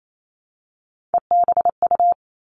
Morse code, heard as a keyed tone, about 28 wpm, 700 Hz. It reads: EBU
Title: Three Challenging Letter-Number Combinations 28wpm